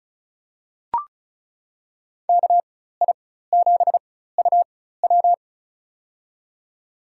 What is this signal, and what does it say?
Morse code, heard as a keyed tone, about 34 wpm, spaced as for 16 wpm, 700 Hz. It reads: KI7UW
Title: USA Call Signs - 5 Characters - 4x Character Spacing - 35wpm